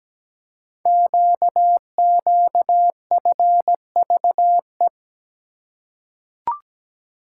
Morse code, heard as a keyed tone, about 17 wpm, 690 Hz. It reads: QQFVE
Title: Code Groups - 5 Letter:Numbers - 17wpm